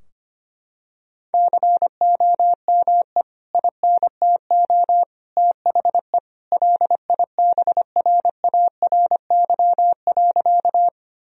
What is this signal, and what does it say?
Morse code, heard as a keyed tone, about 25 wpm, 700 Hz. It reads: COME INTO THE LIBRARY.
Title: Sentences from Top 1000 Words - Review Part 4 - 25wpm